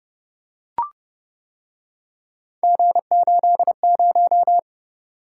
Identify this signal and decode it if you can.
Morse code, heard as a keyed tone, about 29 wpm, 700 Hz. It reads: G80